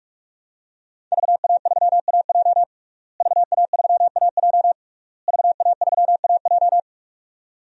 Morse code, heard as a keyed tone, about 41 wpm, 700 Hz. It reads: VA3AJ VA3AJ VA3AJ
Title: DXCC-100 Call Signs - 5 Characters - Repeated 3x - 45wpm